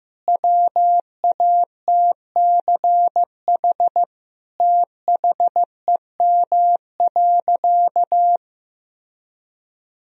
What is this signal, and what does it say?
Morse code, heard as a keyed tone, about 15 wpm, 700 Hz. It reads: WATCH THEM.
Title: Sentences from Top 1000 Words - Review Part 1 - 15wpm